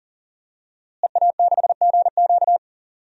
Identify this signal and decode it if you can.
Morse code, heard as a keyed tone, about 40 wpm, 700 Hz. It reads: EA6GQ
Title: DXCC-100 Call Signs - 5 Characters - Repeated 2x - 40wpm